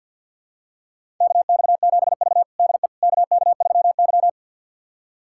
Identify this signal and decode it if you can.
Morse code, heard as a keyed tone, about 52 wpm, 700 Hz. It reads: KX7V DE KK3Y
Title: USA Callsign DE Callsign - 4 Characters - 50wpm